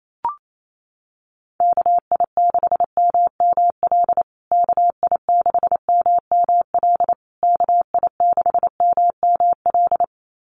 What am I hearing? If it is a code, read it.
Morse code, heard as a keyed tone, about 28 wpm, 700 Hz. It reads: KI6MML KI6MML KI6MML
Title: USA Call Signs Repeated 3x 28wpm